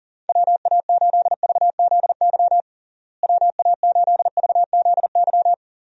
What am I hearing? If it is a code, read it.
Morse code, heard as a keyed tone, about 40 wpm, 700 Hz. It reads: WA8VZY WA8VZY
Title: USA Call Signs - 6 Characters - Repeated 3x - 40wpm